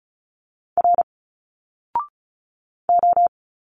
Morse code, heard as a keyed tone, about 35 wpm, 700 Hz. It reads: R O
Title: Single Letters 35wpm